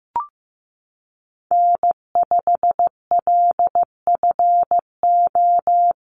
Morse code, heard as a keyed tone, about 15 wpm, 700 Hz. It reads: N5LFO
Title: USA Call Signs Repeated 3x 15wpm